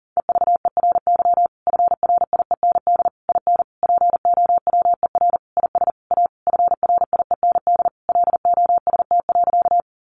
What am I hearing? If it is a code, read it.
Morse code, heard as a keyed tone, about 40 wpm, 700 Hz. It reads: EVERY FRIEND IN POWER IS A FRIEND LOST.
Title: Sentences from Top 1400 Words 40wpm